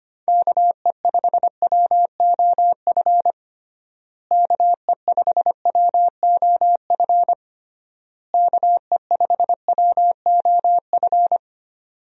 Morse code, heard as a keyed tone, about 25 wpm, 700 Hz. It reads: KE5WOF KE5WOF KE5WOF
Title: USA Call Signs - 6 Characters - Repeated 3x - 25wpm